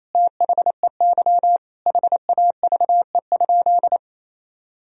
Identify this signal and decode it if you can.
Morse code, heard as a keyed tone, about 28 wpm, 700 Hz. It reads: THEY HAVE?
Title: Sentences from Top 1000 Words - Review Part 3 - 28wpm